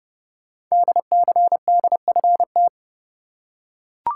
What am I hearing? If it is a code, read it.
Morse code, heard as a keyed tone, about 30 wpm, 700 Hz. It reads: DCDFT
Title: Code Groups - 5 Letters - 30wpm